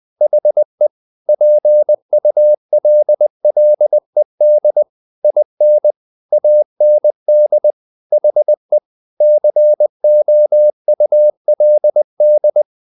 Morse code, heard as a keyed tone, about 20 wpm, 600 Hz. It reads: HE PULLED IN AND HE COULD
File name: the old man and the sea / the old man and the sea part 1 20wpm